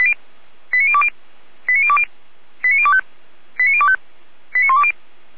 Five-tone selective call (ZVEI), heard as quick runs of five tones, tones of about 70 ms. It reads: 89020 89020 89025 89025 89120